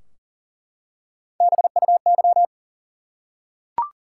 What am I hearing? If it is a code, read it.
Morse code, heard as a keyed tone, about 38 wpm, 700 Hz. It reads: BUY
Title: Three Letter Words - 40wpm